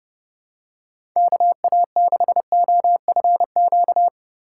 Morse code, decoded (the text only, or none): KA6OFQ